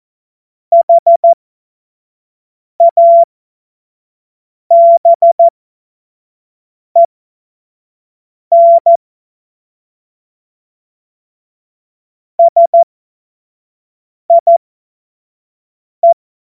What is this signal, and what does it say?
Morse code, heard as a keyed tone, about 14 wpm, spaced as for 5 wpm, 680 Hz. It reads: HABEN SIE
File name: Morsekurs / Witze / Witz 28.November 2023 25 F70